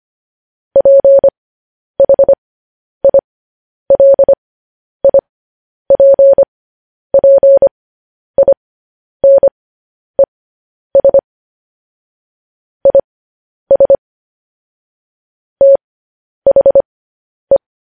Morse code, heard as a keyed tone, about 25 wpm, spaced as for 10 wpm, 550 Hz. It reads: PHILIPPINES IS THE
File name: podcast / News headlines-1x-25x10